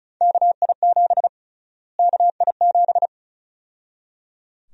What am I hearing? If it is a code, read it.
Morse code, heard as a keyed tone, about 34 wpm, 700 Hz. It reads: KI7 KI7
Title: USA Call Sign Prefixes - Repeated 2x - 35wpm